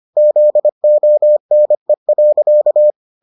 Morse code, heard as a keyed tone, about 25 wpm, 600 Hz. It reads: ZONE.